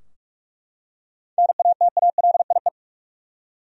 Morse code, heard as a keyed tone, about 41 wpm, 700 Hz. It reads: NATALIE
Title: USA Common Names - 45wpm